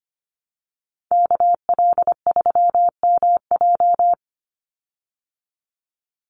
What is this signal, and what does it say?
Morse code, heard as a keyed tone, about 25 wpm, 700 Hz. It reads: KL3MJ
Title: USA Call Signs 25wpm